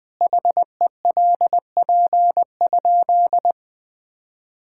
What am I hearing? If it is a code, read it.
Morse code, heard as a keyed tone, about 20 wpm, 700 Hz. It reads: HELP?
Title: Sentences with 4-Word Phrases - 20wpm